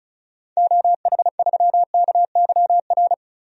Morse code, heard as a keyed tone, about 36 wpm, 700 Hz. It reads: OH3KYR
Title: DXCC-100 Call Signs - 6 Characters - 35wpm